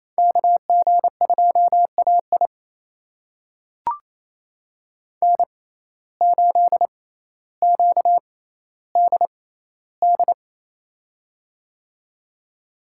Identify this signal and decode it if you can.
Morse code, heard as a keyed tone, about 28 wpm, spaced as for 14 wpm, 700 Hz. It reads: KG2AI N8QDD
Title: USA Call Signs - 5 Characters - 6x Character Spacing - 28wpm